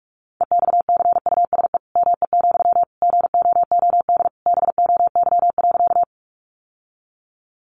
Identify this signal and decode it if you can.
Morse code, heard as a keyed tone, about 44 wpm, 700 Hz. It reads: EXCUSE ME, GOOD BOY.
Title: Sentences from Top 700 Words 45wpm